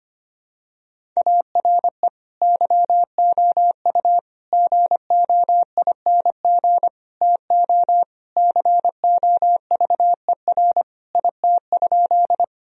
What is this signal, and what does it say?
Morse code, heard as a keyed tone, about 25 wpm, 700 Hz. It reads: ARE YOU GOING TO COVER IT?
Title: Sentences from Top 1000 Words - Review Part 1 - 25wpm